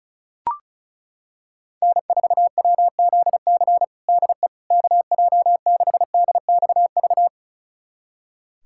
Morse code, heard as a keyed tone, about 34 wpm, 700 Hz. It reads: N4WZC DE KJ6DXV
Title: USA Callsign DE Callsign 35wpm